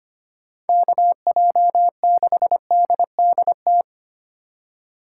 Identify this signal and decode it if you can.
Morse code, heard as a keyed tone, about 29 wpm, 700 Hz. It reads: KJ6DDT
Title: Speed Racing - USA 2x3 Call Signs 25wpm